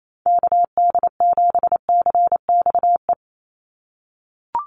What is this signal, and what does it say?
Morse code, heard as a keyed tone, about 28 wpm, 700 Hz. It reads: KD7CXE
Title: USA Call Signs 28wpm